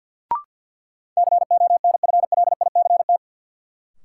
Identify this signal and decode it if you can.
Morse code, heard as a keyed tone, about 47 wpm, 700 Hz. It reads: CONFLICT
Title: Sentences from Top 1300 Words - New Word Drill - 50wpm